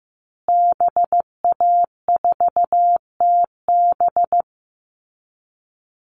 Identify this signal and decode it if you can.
Morse code, heard as a keyed tone, about 15 wpm, 700 Hz. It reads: BA4TB
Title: Call Signs Repeated 3x - 15wpm